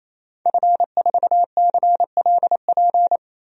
Morse code, heard as a keyed tone, about 27 wpm, 700 Hz. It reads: F4CLP
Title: DXCC-100 Call Signs - 5 Characters - 28wpm